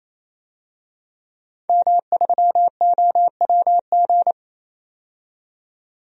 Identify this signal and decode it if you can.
Morse code, heard as a keyed tone, about 28 wpm, 700 Hz. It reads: M3OWG
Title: DXCC-100 Call Signs - 5 Characters - 28wpm